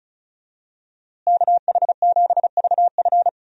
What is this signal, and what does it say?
Morse code, heard as a keyed tone, about 36 wpm, 700 Hz. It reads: KH7VF
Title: USA Call Signs - 5 Characters - Repeated 2x - 35wpm